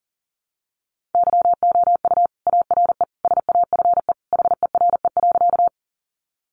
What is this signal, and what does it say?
Morse code, heard as a keyed tone, about 40 wpm, 700 Hz. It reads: YOU ARE SAFE HERE.
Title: Sentences from Top 900 Words 40wpm